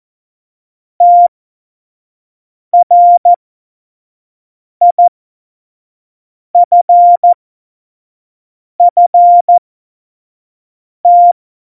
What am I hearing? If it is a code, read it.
Morse code, heard as a keyed tone, about 14 wpm, spaced as for 5 wpm, 700 Hz. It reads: TRIFFT
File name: Morsekurs / Witze / Witz 07 April 2020, 25 F70